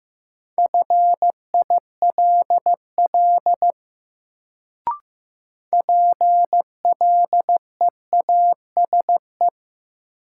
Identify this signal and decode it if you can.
Morse code, heard as a keyed tone, about 15 wpm, 700 Hz. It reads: FILL PLEASE
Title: Sentences from Top 1500 Words - New Word Drill Alt - 15wpm